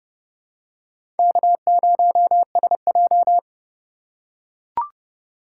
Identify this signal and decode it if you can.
Morse code, heard as a keyed tone, about 30 wpm, 700 Hz. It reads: K0SJ
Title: USA Call Signs - 4 Characters - 4x Character Spacing - 30wpm